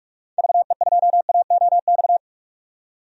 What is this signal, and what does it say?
Morse code, heard as a keyed tone, about 43 wpm, 700 Hz. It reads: VE2AOX
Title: DXCC-100 Call Signs - 6 Characters - 45wpm